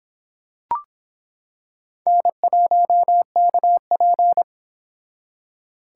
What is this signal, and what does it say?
Morse code, heard as a keyed tone, about 25 wpm, 700 Hz. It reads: N1KP